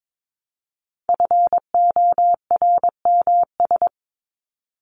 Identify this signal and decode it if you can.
Morse code, heard as a keyed tone, about 22 wpm, 700 Hz. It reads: FORMS
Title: ES:S Suffix - Top 100 Words with Base Words 22wpm